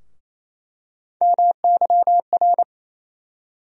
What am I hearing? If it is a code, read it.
Morse code, heard as a keyed tone, about 28 wpm, 700 Hz. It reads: MYR
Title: USA Call Sign Postfixes - 3 Letters 28wpm